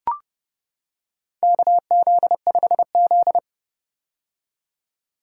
Morse code, heard as a keyed tone, about 32 wpm, 700 Hz. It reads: KZ5Z
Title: K1USN SST QSO Practice - 30wpm